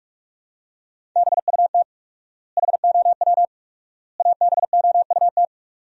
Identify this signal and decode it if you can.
Morse code, heard as a keyed tone, about 46 wpm, 700 Hz. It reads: BUT HOW ABOUT
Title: 4x Word-Spacing - Sentences from Top 100 Words - 45wpm